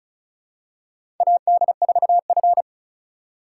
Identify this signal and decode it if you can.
Morse code, heard as a keyed tone, about 36 wpm, 700 Hz. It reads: AD4F
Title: USA Call Signs - 4 Characters - 3x Character Spacing - 35wpm